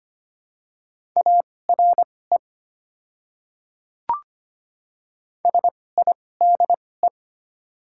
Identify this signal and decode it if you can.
Morse code, heard as a keyed tone, about 25 wpm, spaced as for 18 wpm, 700 Hz. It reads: ARE SIDE
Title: 2x Character Spacing - Top 200 Words - 25wpm